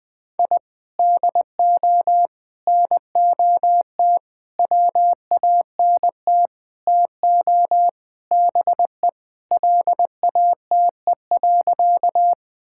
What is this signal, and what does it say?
Morse code, heard as a keyed tone, about 20 wpm, 700 Hz. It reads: I DO NOT WANT TO BE LATE.
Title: Sentences with 5-Word Phrases - Spoken Beginning - 20wpm